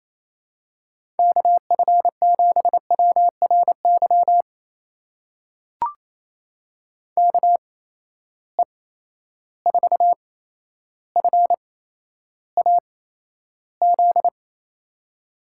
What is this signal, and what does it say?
Morse code, heard as a keyed tone, about 28 wpm, spaced as for 12 wpm, 700 Hz. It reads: KF7WRY KE4FAZ
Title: USA Call Signs - 6 Characters - 8x Character Spacing - 28wpm